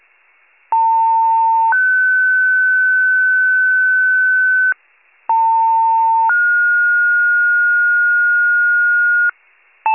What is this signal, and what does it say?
Two-tone page, heard as a one-second tone, then a three-second tone, about 900 then 1500 Hz, about 900 then 1430 Hz, with noise. It is heard from 0.7 to 9.3 s.
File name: pager / 78 FIRE / wcalert